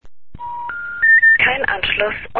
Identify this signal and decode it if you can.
Special information tone, with speech.
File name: deutsch